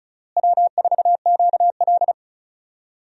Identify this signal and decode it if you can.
Morse code, heard as a keyed tone, about 36 wpm, 700 Hz. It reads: W4QL
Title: USA Call Signs - 4 Characters - Repeated 3x - 35wpm